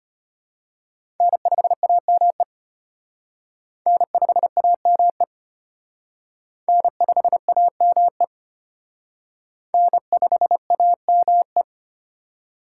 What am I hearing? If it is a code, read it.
Morse code, heard as a keyed tone, about 32 wpm, 700 Hz. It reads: N5AME N5AME N5AME N5AME